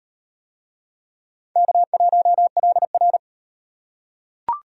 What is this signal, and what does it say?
Morse code, heard as a keyed tone, about 37 wpm, 700 Hz. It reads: K1LR